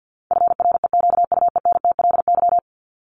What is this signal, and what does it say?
Morse code, heard as a keyed tone, about 50 wpm, 700 Hz. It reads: FREQUENTLY
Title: Top 1001-1250 Words 50wpm